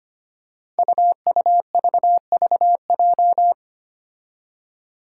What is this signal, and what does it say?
Morse code, heard as a keyed tone, about 25 wpm, 700 Hz. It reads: UUVVJ
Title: Five Challenging Letter-Number Combinations 25wpm